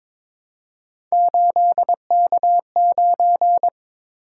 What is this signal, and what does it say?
Morse code, heard as a keyed tone, about 22 wpm, 700 Hz. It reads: 8K9
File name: practice / 22wpm / Three Letter-Number Combinations - 22wpm